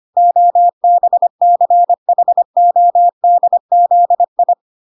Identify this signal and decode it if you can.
Morse code, heard as a keyed tone, about 25 wpm, 700 Hz. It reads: OBCHODZI
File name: recordings / 25wpm0000